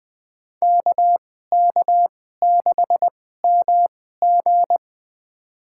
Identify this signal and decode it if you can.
Morse code, heard as a keyed tone, about 21 wpm, spaced as for 15 wpm, 700 Hz. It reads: KK6MG
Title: USA Call Signs - 5 Characters - 2x Character Spacing - 20wpm